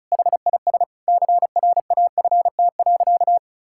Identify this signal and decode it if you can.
Morse code, heard as a keyed tone, about 36 wpm, 700 Hz. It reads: HIS CRAFT.
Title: Sentences from Top 1000 Words - Review Part 6 - 35wpm